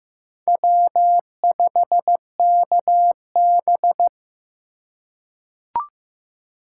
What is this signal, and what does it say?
Morse code, heard as a keyed tone, about 15 wpm, 700 Hz. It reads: W5KB
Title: USA Call Signs - 4x Character Spacing - 15wpm